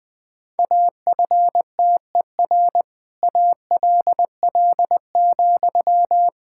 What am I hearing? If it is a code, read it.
Morse code, heard as a keyed tone, about 21 wpm, 700 Hz. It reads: AFTER ALL,
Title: Sentences from Top 1000 Words - Review Part 6 - 20wpm